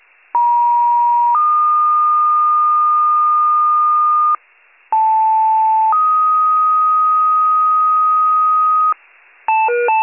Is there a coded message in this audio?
Yes: two-tone page from 0.3 to 8.9 s, heard as a one-second tone, then a three-second tone, about 950 then 1220 Hz, about 880 then 1250 Hz.